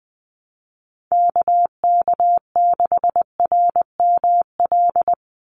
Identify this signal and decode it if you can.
Morse code, heard as a keyed tone, about 20 wpm, 700 Hz. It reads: KK6RML
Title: USA Call Signs 20wpm